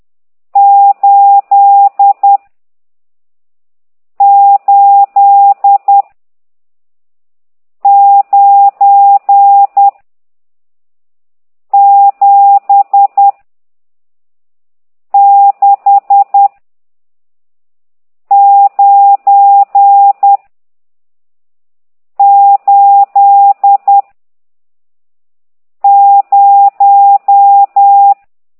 Morse code, heard as a keyed tone, about 10 wpm, 800 Hz. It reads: 8 8 9 7 6 9 8 0
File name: ct2idw / exerc14